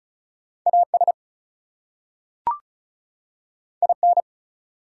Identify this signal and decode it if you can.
Morse code, heard as a keyed tone, about 36 wpm, 700 Hz. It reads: AS IN
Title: Two Letter Words - 35wpm